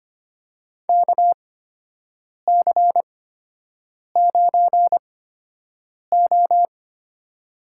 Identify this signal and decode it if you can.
Morse code, heard as a keyed tone, about 25 wpm, spaced as for 7 wpm, 700 Hz. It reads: KC9O
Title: USA Call Signs - 6 Characters - 8x Character Spacing - 25wpm